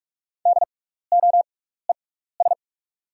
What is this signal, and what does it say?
Morse code, heard as a keyed tone, about 44 wpm, spaced as for 15 wpm, 690 Hz. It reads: DOES
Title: 6x Character Spacing - Top 500 Words - 45wpm